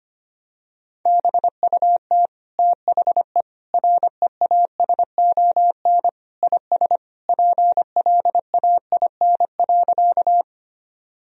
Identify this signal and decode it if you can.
Morse code, heard as a keyed tone, about 25 wpm, 700 Hz. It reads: BUT THE REASON IS PLAIN.
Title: Sentences from Top 1000 Words - Review Part 2 - 25wpm